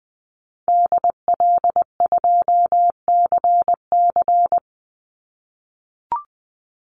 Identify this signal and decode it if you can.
Morse code, heard as a keyed tone, about 19 wpm, 700 Hz. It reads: DL2CC
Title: Call Signs - 20wpm